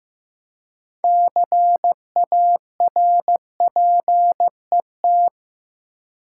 Morse code, heard as a keyed tone, about 16 wpm, 700 Hz. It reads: CARPET
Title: Six Letter Words - 15wpm